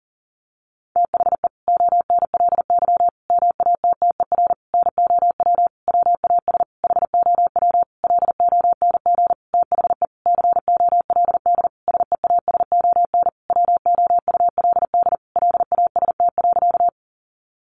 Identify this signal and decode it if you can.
Morse code, heard as a keyed tone, about 40 wpm, 700 Hz. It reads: THE ONLY MATTER NOW WAS HOW LONG THE COLD SEASON WOULD LAST.
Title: Sentences from Top 1400 Words 40wpm